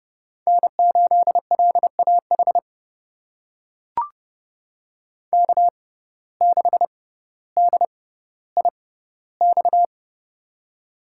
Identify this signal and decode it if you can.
Morse code, heard as a keyed tone, about 30 wpm, spaced as for 15 wpm, 700 Hz. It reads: N8LAH K6DIX
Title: USA Call Signs - 5 Characters - 6x Character Spacing - 30wpm